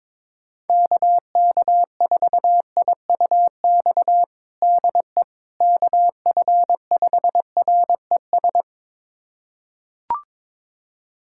Morse code, heard as a keyed tone, about 22 wpm, 700 Hz. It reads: KK4IUX DE KF5RES
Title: USA Callsign DE Callsign 22wpm